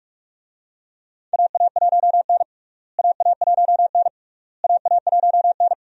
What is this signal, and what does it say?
Morse code, heard as a keyed tone, about 42 wpm, 700 Hz. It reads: AA1N AA1N AA1N
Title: USA Call Signs - 4 Characters - Repeated 3x - 45wpm